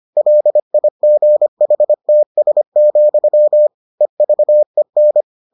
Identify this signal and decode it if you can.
Morse code, heard as a keyed tone, about 25 wpm, 600 Hz. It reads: LIGHTS, EVEN